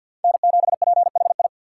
Morse code, heard as a keyed tone, about 53 wpm, 690 Hz. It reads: N7PHI
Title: K1USN SST QSO Practice - 50wpm